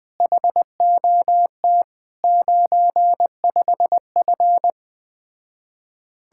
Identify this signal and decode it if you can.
Morse code, heard as a keyed tone, about 20 wpm, 700 Hz. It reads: HOT 95F
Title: Speed-Racing Common QSO Phrases - 17wpm